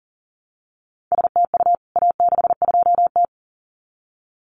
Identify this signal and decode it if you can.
Morse code, heard as a keyed tone, about 39 wpm, 700 Hz. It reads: STU A62T
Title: CWT Exchange 40wpm